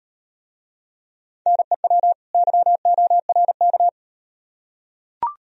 Morse code, heard as a keyed tone, about 38 wpm, 700 Hz. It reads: NEW YORK